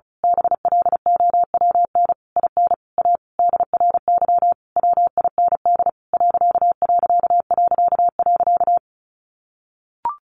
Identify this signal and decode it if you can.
Morse code, heard as a keyed tone, about 34 wpm, 700 Hz. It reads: BLOWN IN A DRY WIND ....
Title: A Sound Of Thunder 35wpm